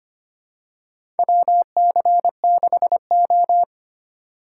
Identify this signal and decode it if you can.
Morse code, heard as a keyed tone, about 25 wpm, 700 Hz. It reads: WC6O
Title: USA Call Signs - 4 Characters - 4x Character Spacing - 25wpm